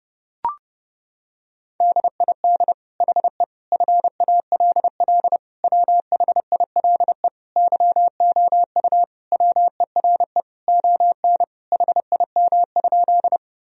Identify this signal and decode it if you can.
Morse code, heard as a keyed tone, about 30 wpm, 700 Hz. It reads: DID HE FALL WHILE YOU WERE ON HIM?
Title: Sentences from Top 1000 Words - Review Part 2 - 30wpm